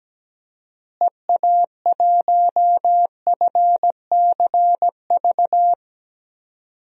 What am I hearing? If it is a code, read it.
Morse code, heard as a keyed tone, about 17 wpm, 700 Hz. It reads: EA1FCV
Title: DXCC-100 Call Signs - 6 Characters - Repeated 2x - 17wpm